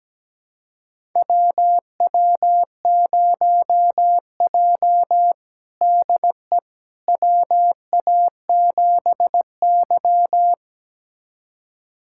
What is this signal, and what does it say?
Morse code, heard as a keyed tone, about 17 wpm, 700 Hz. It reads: WW0J DE WA7Y
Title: USA Callsign DE Callsign - 4 Characters - 17wpm